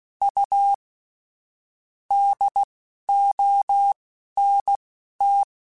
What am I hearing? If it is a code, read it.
Morse code, heard as a keyed tone, about 16 wpm, spaced as for 11 wpm, 800 Hz. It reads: U DONT
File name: QSOs / QSOs-mp3s / WD0BGP 2@10